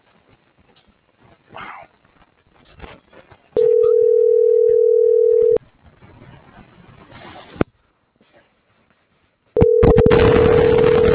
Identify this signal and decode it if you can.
Ringback tone (North America), 440 + 480 Hz.